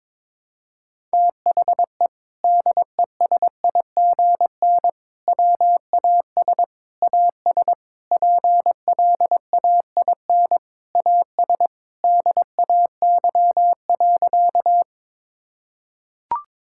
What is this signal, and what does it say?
Morse code, heard as a keyed tone, about 22 wpm, 700 Hz. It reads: THE DESIGN WAS AS PLAIN AS DAY.
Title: Sentences from Top 1000 Words - Review Part 5 - 22wpm